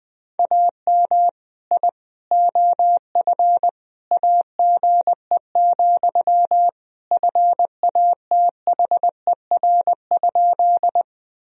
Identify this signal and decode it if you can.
Morse code, heard as a keyed tone, about 20 wpm, 700 Hz. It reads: AM I OF AGE, FATHER?
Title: Sentences from Top 1000 Words - Review Part 3 - 20wpm